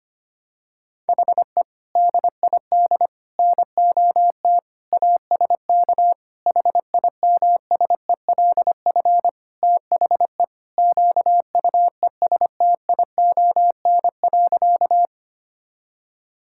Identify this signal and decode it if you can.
Morse code, heard as a keyed tone, about 25 wpm, 700 Hz. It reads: HE DID NOT ASK HIMSELF THE QUESTION.
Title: Sentences from Top 1600 Words 25wpm